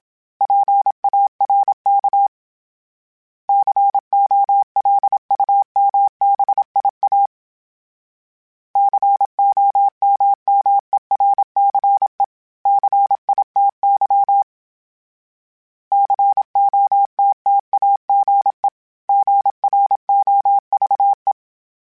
Morse code, heard as a keyed tone, about 27 wpm, 800 Hz. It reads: PARK COLUMBIA COMMERCE CITY COTTAGE GROVE
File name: fromSMs / FromSMs mp3s / Cities@25